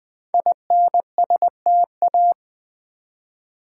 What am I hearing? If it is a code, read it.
Morse code, heard as a keyed tone, about 20 wpm, 700 Hz. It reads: INSTA
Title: Build A Word - 5 Letters At A Time - 12-Letter Words - 20wpm